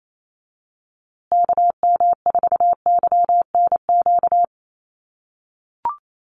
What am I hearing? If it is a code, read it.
Morse code, heard as a keyed tone, about 28 wpm, 700 Hz. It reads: KM4YNQ
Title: USA Call Signs Repeated 2x 28wpm